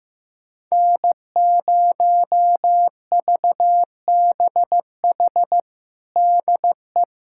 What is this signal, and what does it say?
Morse code, heard as a keyed tone, about 15 wpm, 700 Hz. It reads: N0VBH DE
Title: USA Callsign DE Callsign 15wpm